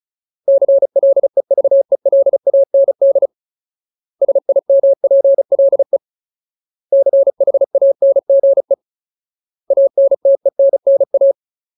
Morse code, heard as a keyed tone, about 34 wpm, 550 Hz. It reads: CLEVELAND SIMPLE CHANGE ANTENNA
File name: Ultimate Word List / 35WPM Ultimate